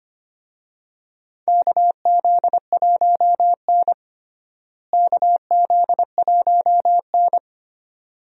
Morse code, heard as a keyed tone, about 25 wpm, 700 Hz. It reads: KZ1N KZ1N
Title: USA Call Signs - 4 Characters - Repeated 2x - 25wpm